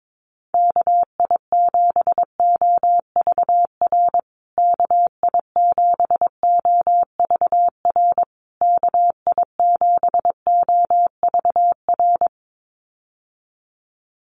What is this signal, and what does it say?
Morse code, heard as a keyed tone, about 22 wpm, 700 Hz. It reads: KI7OVR KI7OVR KI7OVR
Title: USA Call Signs Repeated 3x 22wpm